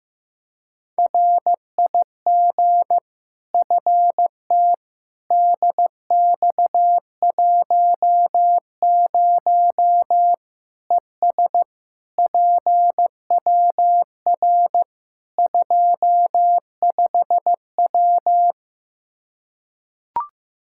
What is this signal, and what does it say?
Morse code, heard as a keyed tone, about 15 wpm, 700 Hz. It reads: RIG FT DX10 ES PWR 25W
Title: Practice QSO - Exchange 2 - Info - 15wpm